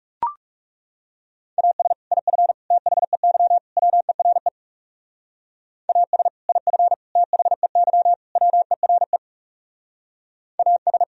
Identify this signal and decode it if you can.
Morse code, heard as a keyed tone, about 36 wpm, 700 Hz. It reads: AS IF THEY WERE AS IF THEY WERE AS
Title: Speed-Racing - Sets of 4 Words 30wpm